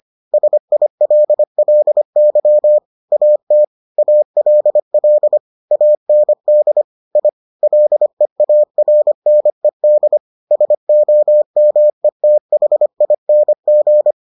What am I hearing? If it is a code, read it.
Morse code, heard as a keyed tone, about 25 wpm, 600 Hz. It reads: SILLY AT ALL AND I LEARNED SOMETHING